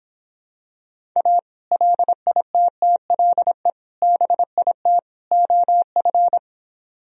Morse code, heard as a keyed tone, about 26 wpm, 700 Hz. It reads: A LITTLE BIT OF